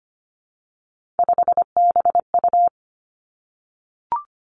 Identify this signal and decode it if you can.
Morse code, heard as a keyed tone, about 25 wpm, 700 Hz. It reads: 5BU
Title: Three Challenging Letter-Number Combinations 25wpm